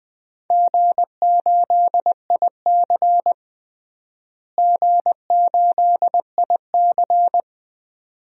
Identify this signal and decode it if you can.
Morse code, heard as a keyed tone, about 20 wpm, 700 Hz. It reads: G8IC G8IC
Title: DXCC-100 Call Signs - 4 Characters - Repeated 2x - 20wpm